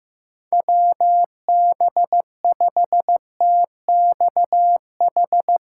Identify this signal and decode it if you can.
Morse code, heard as a keyed tone, about 15 wpm, 700 Hz. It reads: WB5TXH
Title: USA Call Signs - 6 Characters - 4x Character Spacing - 15wpm